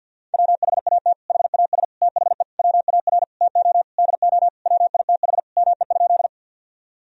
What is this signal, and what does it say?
Morse code, heard as a keyed tone, about 50 wpm, 700 Hz. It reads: WHAT HAS THE WAR TO DO WITH ME?
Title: Sentences from Top 1000 Words - Review Part 2 - 50wpm